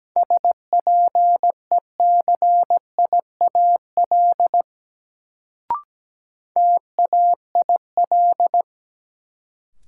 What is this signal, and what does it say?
Morse code, heard as a keyed tone, about 17 wpm, 700 Hz. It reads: SPECIAL TAIL
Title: Sentences from Top 1000 Words - Review Part 3 - New Word Drill - 17wpm